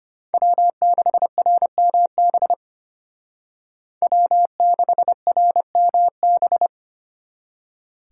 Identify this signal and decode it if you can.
Morse code, heard as a keyed tone, about 33 wpm, 700 Hz. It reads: W6RMB W6RMB